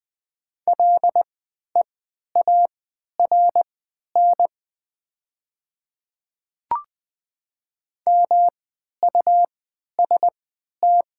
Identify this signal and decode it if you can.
Morse code, heard as a keyed tone, about 20 wpm, spaced as for 11 wpm, 700 Hz. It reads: LEARN MUST
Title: 3x Character Spacing - Top 300 Words - 20wpm